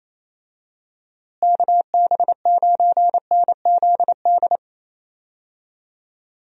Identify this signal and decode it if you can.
Morse code, heard as a keyed tone, about 28 wpm, 700 Hz. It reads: KB9NZD
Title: USA Call Signs - 6 Characters - Repeated 2x - 28wpm